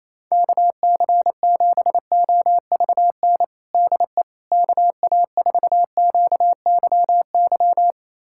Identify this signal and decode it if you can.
Morse code, heard as a keyed tone, about 28 wpm, 700 Hz. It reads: KC7OVN DE KA4QYY
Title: USA Callsign DE Callsign 28wpm